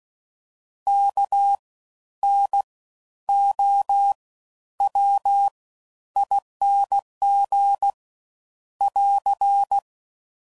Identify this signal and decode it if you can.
Morse code, heard as a keyed tone, about 16 wpm, spaced as for 11 wpm, 800 Hz. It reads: KNOWING <AR>
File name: Suffixes / Special INGs / Fast Ings